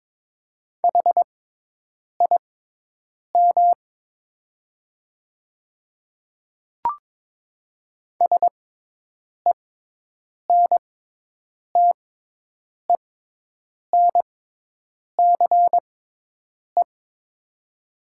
Morse code, heard as a keyed tone, about 22 wpm, spaced as for 8 wpm, 700 Hz. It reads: HIM SENTENCE
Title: 6x Character Spacing - Top 200 Words - 22wpm